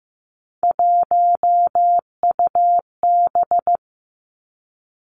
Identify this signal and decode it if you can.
Morse code, heard as a keyed tone, about 15 wpm, 700 Hz. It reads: 1UB
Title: Three Challenging Letter-Number Combinations 15wpm